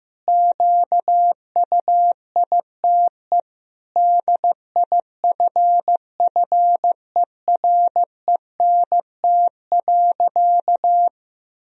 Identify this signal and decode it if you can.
Morse code, heard as a keyed tone, about 15 wpm, 700 Hz. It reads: QUITE DIFFERENT.
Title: Sentences from Top 1000 Words - Review Part 3 - 15wpm